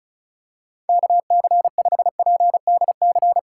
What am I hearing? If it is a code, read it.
Morse code, heard as a keyed tone, about 36 wpm, 700 Hz. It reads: KC5PDC
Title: USA Call Signs - 6 Characters - 3x Character Spacing - 35wpm